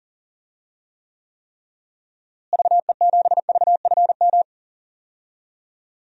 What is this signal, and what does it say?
Morse code, heard as a keyed tone, about 40 wpm, 700 Hz. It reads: VE7VFM
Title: DXCC-100 Call Signs - 6 Characters - Repeated 3x - 40wpm